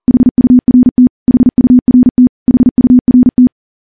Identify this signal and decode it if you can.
Morse code, heard as a keyed tone, about 43 wpm, 250 Hz. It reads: HURT HURT HURT
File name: morse / morse-training-40WPM